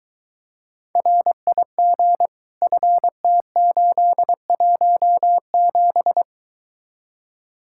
Morse code, heard as a keyed tone, about 23 wpm, 700 Hz. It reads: RIG FT817